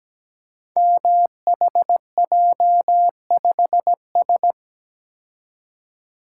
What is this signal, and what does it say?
Morse code, heard as a keyed tone, about 17 wpm, 700 Hz. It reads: MHJ5S
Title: Code Groups - 5 Letter:Numbers - 17wpm